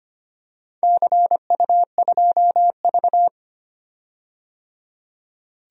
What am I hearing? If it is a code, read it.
Morse code, heard as a keyed tone, about 25 wpm, 700 Hz. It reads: CU2V
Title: DXCC-100 Call Signs - 4 Characters - 25wpm